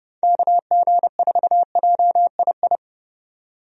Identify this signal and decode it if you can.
Morse code, heard as a keyed tone, about 31 wpm, 700 Hz. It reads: KG4JII
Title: USA Call Signs - 6 Characters - 30wpm